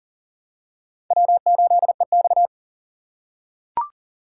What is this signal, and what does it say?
Morse code, heard as a keyed tone, about 41 wpm, 700 Hz. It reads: W8EX
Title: USA Call Signs - 4 Characters - 3x Character Spacing - 40wpm